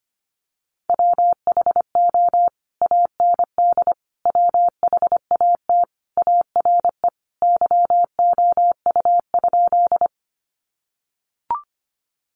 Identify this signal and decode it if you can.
Morse code, heard as a keyed tone, about 25 wpm, 700 Hz. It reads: WHO AND WHAT ARE YOU?
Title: Sentences from Top 100 Words 25wpm